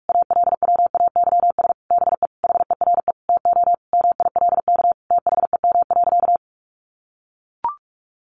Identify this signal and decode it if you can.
Morse code, heard as a keyed tone, about 44 wpm, 700 Hz. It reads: ALWAYS BE HERE TO MILK THEM.
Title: Sentences from Top 1500 Words 45wpm